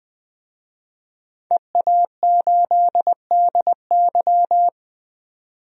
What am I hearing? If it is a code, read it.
Morse code, heard as a keyed tone, about 20 wpm, 700 Hz. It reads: EA8DY
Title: DXCC-100 Call Signs - 5 Characters - 20wpm